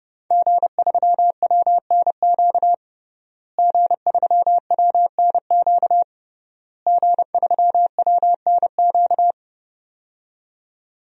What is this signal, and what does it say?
Morse code, heard as a keyed tone, about 31 wpm, 700 Hz. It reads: G3WNQ G3WNQ G3WNQ
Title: DXCC-100 Call Signs - 5 Characters - Repeated 3x - 30wpm